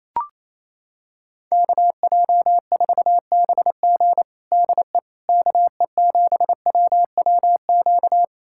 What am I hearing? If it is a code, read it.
Morse code, heard as a keyed tone, about 28 wpm, 700 Hz. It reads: KJ4BG DE KE7WWQ
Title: USA Callsign DE Callsign 28wpm